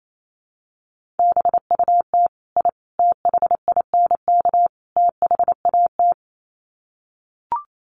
Morse code, heard as a keyed tone, about 28 wpm, 700 Hz. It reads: BUT I THINK THAT
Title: Sets of 4 Words 28wpm